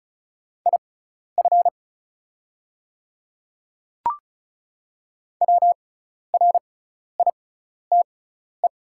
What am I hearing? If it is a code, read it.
Morse code, heard as a keyed tone, about 36 wpm, spaced as for 12 wpm, 700 Hz. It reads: IF WRITE
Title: 6x Character Spacing - Top 200 Words - 35wpm